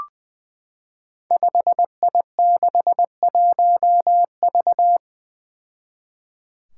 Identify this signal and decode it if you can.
Morse code, heard as a keyed tone, about 20 wpm, 700 Hz. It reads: 5I61V
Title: Five Challenging Letter-Number Combinations 20wpm